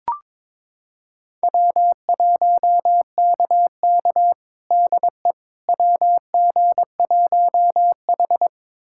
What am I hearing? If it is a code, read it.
Morse code, heard as a keyed tone, about 22 wpm, 700 Hz. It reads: W1KK DE WG1H
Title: USA Callsign DE Callsign - 4 Characters - 22wpm